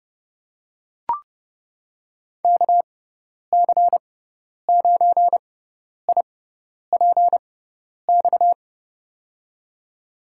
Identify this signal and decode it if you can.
Morse code, heard as a keyed tone, about 30 wpm, spaced as for 10 wpm, 700 Hz. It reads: KC9IPX
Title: USA Call Signs - 6 Characters - 6x Character Spacing - 30wpm